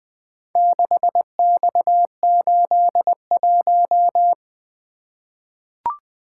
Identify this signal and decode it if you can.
Morse code, heard as a keyed tone, about 20 wpm, 700 Hz. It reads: 6X81
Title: Four Letter-Number Combinations - 20wpm